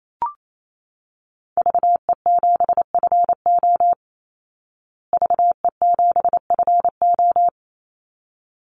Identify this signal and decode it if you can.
Morse code, heard as a keyed tone, about 28 wpm, 700 Hz. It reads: VE7FO VE7FO
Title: Call Signs Repeated 3x - 28wpm